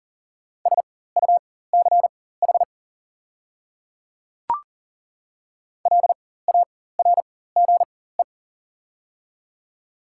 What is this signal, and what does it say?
Morse code, heard as a keyed tone, about 40 wpm, spaced as for 19 wpm, 700 Hz. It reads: SUCH LARGE
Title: 4x Character Spacing - Top 200 Words - 40wpm